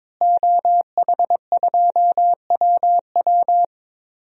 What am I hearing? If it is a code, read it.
Morse code, heard as a keyed tone, about 22 wpm, 700 Hz. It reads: OH2WW
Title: DXCC-100 Call Signs - 5 Characters - Repeated 3x - 22wpm